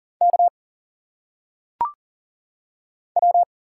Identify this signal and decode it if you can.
Morse code, heard as a keyed tone, about 40 wpm, 700 Hz. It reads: K W
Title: Single Letters - Phonetic Pronunciation 40wpm